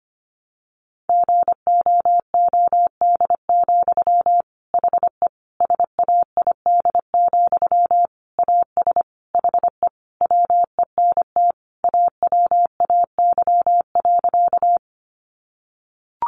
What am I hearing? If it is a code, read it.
Morse code, heard as a keyed tone, about 25 wpm, 700 Hz. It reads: GOOD, HE SAID, AS HE WENT AWAY.
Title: Sentences from Top 600 Words 25wpm